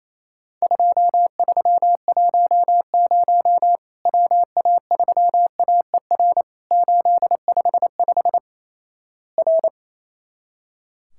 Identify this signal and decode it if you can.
Morse code, heard as a keyed tone, about 28 wpm, 700 Hz. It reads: 2310 WA3AER 855 R
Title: Worked All Europe DX Contest - QTC Exchange - 28wpm